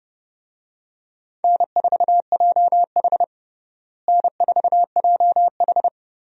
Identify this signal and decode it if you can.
Morse code, heard as a keyed tone, about 32 wpm, 700 Hz. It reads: N4JH N4JH
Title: USA Call Signs - 4 Characters - Repeated 3x - 30wpm